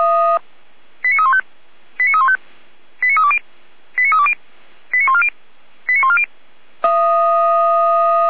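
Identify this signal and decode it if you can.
Five-tone selective call (ZVEI), heard as quick runs of five tones, tones of about 70 ms. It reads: 89315 89315 89320 89320 89140 89140
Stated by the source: ZVEI five-tone selective call received on an SDR.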